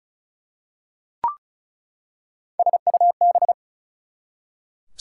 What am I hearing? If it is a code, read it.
Morse code, heard as a keyed tone, about 34 wpm, 700 Hz. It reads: SUB
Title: Prefixes and Suffixes 35wpm